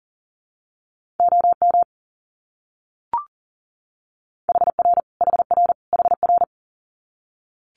Morse code, heard as a keyed tone, about 40 wpm, 700 Hz. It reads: OM HR HR HR
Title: Abbreviations Repeated 3x - 40wpm